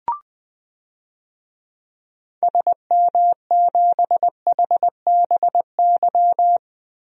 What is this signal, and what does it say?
Morse code, heard as a keyed tone, about 20 wpm, 700 Hz. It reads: SM7HBY
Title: DXCC-100 Call Signs - 6 Characters - Repeated 3x - 20wpm